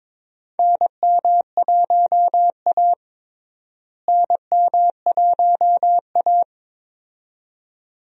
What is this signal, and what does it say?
Morse code, heard as a keyed tone, about 22 wpm, 700 Hz. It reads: NM1A NM1A
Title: USA Call Signs - 4 Characters - Repeated 2x - 22wpm